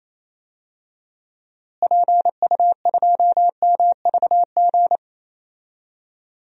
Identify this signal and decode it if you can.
Morse code, heard as a keyed tone, about 28 wpm, 700 Hz. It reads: PU2MVG
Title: DXCC-100 Call Signs - 6 Characters - Repeated 2x - 28wpm